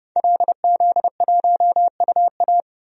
Morse code, heard as a keyed tone, about 31 wpm, 700 Hz. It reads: LZ1UA
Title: DXCC-100 Call Signs - 5 Characters - 30wpm